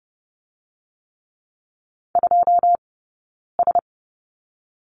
Morse code, heard as a keyed tone, about 30 wpm, 700 Hz. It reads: 2 S
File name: Lesson 19d - Letter w - Two Characters 30wpm